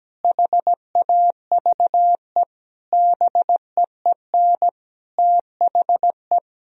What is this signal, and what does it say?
Morse code, heard as a keyed tone, about 17 wpm, 700 Hz. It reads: HAVE BEEN THE
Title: Sentences from Top 1000 Words - Review Part 2 - 17wpm